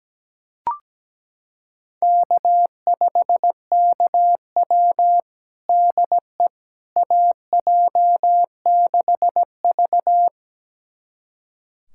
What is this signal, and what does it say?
Morse code, heard as a keyed tone, about 17 wpm, 700 Hz. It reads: K5KW DE AJ6V
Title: USA Callsign DE Callsign - 4 Characters - 17wpm